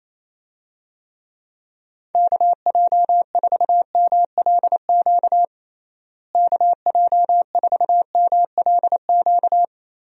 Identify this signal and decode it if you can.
Morse code, heard as a keyed tone, about 28 wpm, 700 Hz. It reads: KJ4MLQ KJ4MLQ
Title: USA Call Signs - 6 Characters - Repeated 3x - 28wpm